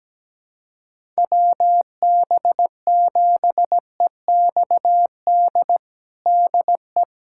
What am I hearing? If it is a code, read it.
Morse code, heard as a keyed tone, about 17 wpm, 700 Hz. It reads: WB7EXD DE
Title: USA Callsign DE Callsign 17wpm